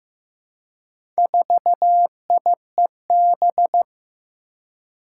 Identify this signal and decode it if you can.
Morse code, heard as a keyed tone, about 15 wpm, 700 Hz. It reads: 4IEB
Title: Four Challenging Letter-Number Combinations 15wpm